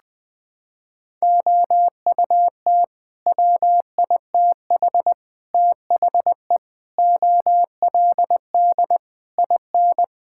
Morse code, heard as a keyed tone, about 21 wpm, 700 Hz. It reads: OUT WITH THE OLD IN
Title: Sets of 8 Words - 20wpm